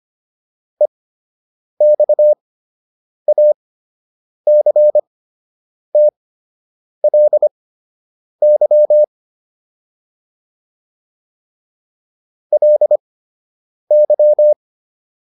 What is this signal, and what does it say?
Morse code, heard as a keyed tone, about 25 wpm, spaced as for 8 wpm, 600 Hz. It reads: EXACTLY LY